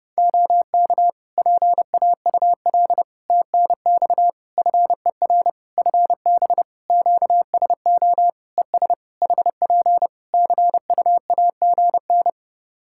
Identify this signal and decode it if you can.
Morse code, heard as a keyed tone, about 28 wpm, 700 Hz. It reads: OK PAUL TNX FER FB QSO ES HP CUAGN
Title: K1USN SST QSO Practice 30wpm